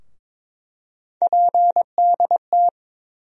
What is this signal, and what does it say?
Morse code, heard as a keyed tone, about 23 wpm, 700 Hz. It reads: PDT
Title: USA Call Sign Postfixes - 3 Letters 22wpm